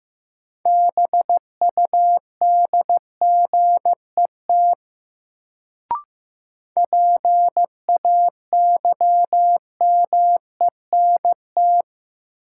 Morse code, heard as a keyed tone, about 15 wpm, 700 Hz. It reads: BUDGET PAYMENT
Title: Sentences from Top 1200 Words - New Word Drill - 15wpm